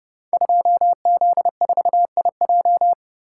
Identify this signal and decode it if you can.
Morse code, heard as a keyed tone, about 28 wpm, 700 Hz. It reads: 2Z4IJ
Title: Code Groups - 5 Letter:Numbers - 30wpm